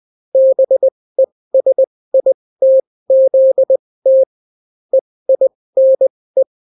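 Morse code, heard as a keyed tone, about 20 wpm, spaced as for 16 wpm, 530 Hz. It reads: BESITZT EINE
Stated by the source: Daily Morse weather and top stories at 16 wpm, clean without QRN.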